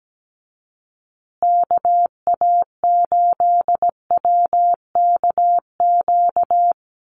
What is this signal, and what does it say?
Morse code, heard as a keyed tone, about 17 wpm, 700 Hz. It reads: KA8WKQ
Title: USA Call Signs Repeated 2x 17wpm